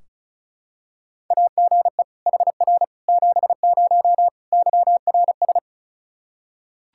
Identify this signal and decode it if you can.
Morse code, heard as a keyed tone, about 34 wpm, 700 Hz. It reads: AGE HR 70 YRS
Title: Practice QSO - Exchange 3 - More Info - 35wpm